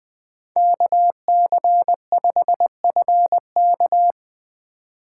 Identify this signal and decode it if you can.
Morse code, heard as a keyed tone, about 20 wpm, 700 Hz. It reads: KC5FK
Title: Speed Racing - USA 2x2 Call Signs 17wpm